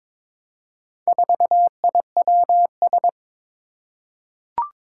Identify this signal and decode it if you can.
Morse code, heard as a keyed tone, about 23 wpm, 700 Hz. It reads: 4IWS
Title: Four Challenging Letter-Number Combinations 22wpm